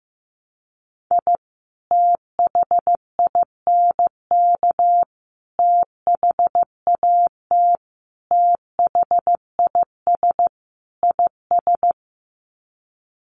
Morse code, heard as a keyed tone, about 15 wpm, 700 Hz. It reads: I THINK THAT THIS IS
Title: Sets of 5 Words Encore2 15wpm